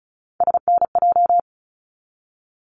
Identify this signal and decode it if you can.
Morse code, heard as a keyed tone, about 34 wpm, 700 Hz. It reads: SNJ
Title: ARRL Sections 35wpm